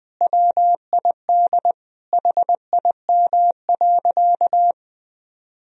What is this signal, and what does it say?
Morse code, heard as a keyed tone, about 20 wpm, 700 Hz. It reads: WID HIM.
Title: Sentences from Top 400 Words and Abbreviations 20wpm